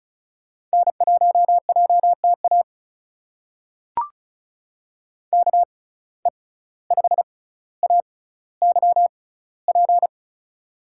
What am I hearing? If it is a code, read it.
Morse code, heard as a keyed tone, about 34 wpm, spaced as for 17 wpm, 700 Hz. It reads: N1JTA KE5AYP
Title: USA Call Signs - 6x Character Spacing - 35wpm